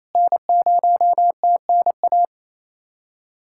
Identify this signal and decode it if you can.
Morse code, heard as a keyed tone, about 27 wpm, 700 Hz. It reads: N0TNA
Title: USA Callsign DE Callsign - 5 Characters - 28wpm